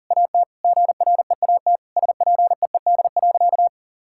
Morse code, heard as a keyed tone, about 40 wpm, 700 Hz. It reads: AT GREAT SPEED.
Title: Sentences from Top 1000 Words - Review Part 5 - 40wpm